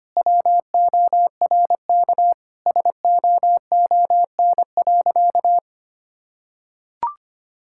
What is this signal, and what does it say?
Morse code, heard as a keyed tone, about 25 wpm, 700 Hz. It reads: WORK SOON.
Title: Sentences from Top 1000 Words - Review Part 6 - 25wpm